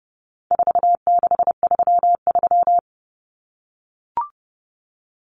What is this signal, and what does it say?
Morse code, heard as a keyed tone, about 30 wpm, 700 Hz. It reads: 4633
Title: Common Cut Numbers - 4-Digit 30wpm